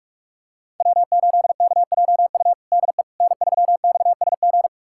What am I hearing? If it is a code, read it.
Morse code, heard as a keyed tone, about 46 wpm, 700 Hz. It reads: W8KJU DE N3XSG
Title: USA Callsign DE Callsign - 5 Characters - 45wpm